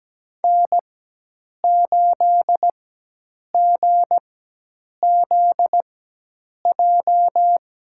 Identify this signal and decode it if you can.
Morse code, heard as a keyed tone, about 17 wpm, spaced as for 8 wpm, 700 Hz. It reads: N8GZJ
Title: USA Call Signs - 5 Characters - 4x Character Spacing - 17wpm